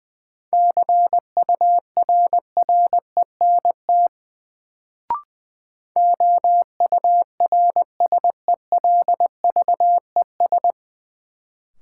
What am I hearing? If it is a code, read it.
Morse code, heard as a keyed tone, about 20 wpm, 700 Hz. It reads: CURRENT OURSELVES
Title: Sentences from Top 1600 Words - New Word Drill Alt - 20wpm